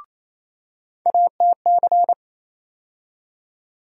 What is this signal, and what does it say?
Morse code, heard as a keyed tone, about 28 wpm, 700 Hz. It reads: ATC
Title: USA Call Sign Postfixes - 3 Letters 28wpm